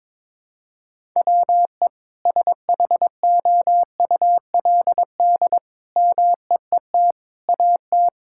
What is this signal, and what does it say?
Morse code, heard as a keyed tone, about 22 wpm, 700 Hz. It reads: WE SHOULD MEET AT
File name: Sentences with 7-Word Phrases - 22wpm